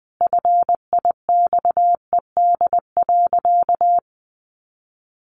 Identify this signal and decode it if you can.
Morse code, heard as a keyed tone, about 20 wpm, 700 Hz. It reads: FIXED.
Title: Sentences from Top 1100 Words 20wpm